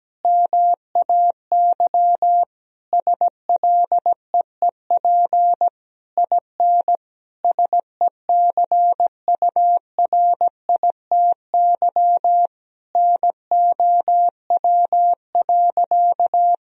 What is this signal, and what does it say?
Morse code, heard as a keyed tone, about 17 wpm, 700 Hz. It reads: MAY SLEEP IN SECURITY NOW.
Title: Sentences from Top 1000 Words - Review Part 4 - 17wpm